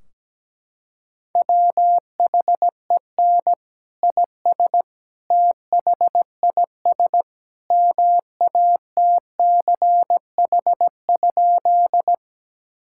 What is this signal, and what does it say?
Morse code, heard as a keyed tone, about 17 wpm, 700 Hz. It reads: WHEN IS THIS MATCH?
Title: Sentences from Top 1600 Words 17wpm